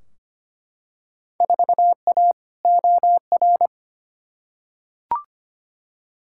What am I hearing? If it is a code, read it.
Morse code, heard as a keyed tone, about 25 wpm, 700 Hz. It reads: 4A OR
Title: Field Day Exchanges Encore - 25wpm